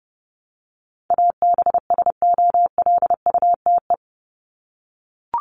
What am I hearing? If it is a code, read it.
Morse code, heard as a keyed tone, about 28 wpm, 700 Hz. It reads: ABSOLUTE
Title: LY Suffix - Top 100 Words with Base Words 30wpm